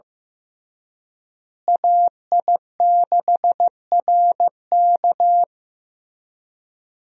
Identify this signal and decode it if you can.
Morse code, heard as a keyed tone, about 15 wpm, 700 Hz. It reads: AI6RK
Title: USA Call Signs - 5 Characters - Repeated 3x - 15wpm